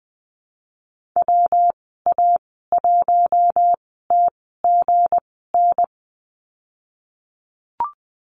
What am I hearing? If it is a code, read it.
Morse code, heard as a keyed tone, about 20 wpm, spaced as for 15 wpm, 700 Hz. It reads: WA1TGN
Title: 186 Lesson 30g - Letter b - Call Signs 20wpm